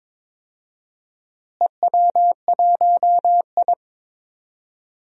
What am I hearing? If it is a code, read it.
Morse code, heard as a keyed tone, about 22 wpm, 700 Hz. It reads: EW1I
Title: DXCC-100 Call Signs - 4 Characters - Repeated 2x - 22wpm